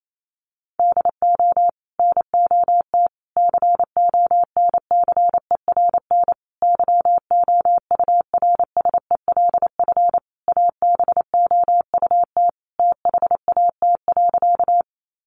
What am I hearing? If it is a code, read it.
Morse code, heard as a keyed tone, about 28 wpm, 700 Hz. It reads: DO NOT CONCERN YOURSELF ABOUT THAT.
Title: Sentences from Top 1000 Words 28wpm